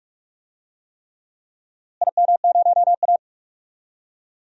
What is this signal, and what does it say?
Morse code, heard as a keyed tone, about 46 wpm, 690 Hz. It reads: IM0A
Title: DXCC-100 Call Signs - 4 Characters - Repeated 3x - 45wpm